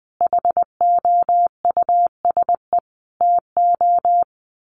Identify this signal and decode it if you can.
Morse code, heard as a keyed tone, about 20 wpm, 700 Hz. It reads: HOUSE TO
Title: Sentences from Top 1200 Words 20wpm